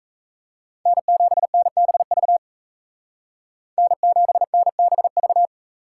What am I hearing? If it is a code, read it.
Morse code, heard as a keyed tone, about 33 wpm, 700 Hz. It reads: N7NBV N7NBV
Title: Speed Racing - USA 1x3 Call Signs 28wpm